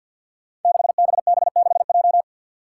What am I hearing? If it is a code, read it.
Morse code, heard as a keyed tone, about 52 wpm, 700 Hz. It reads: 6BB6J
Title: Five Challenging Letter-Number Combinations 50wpm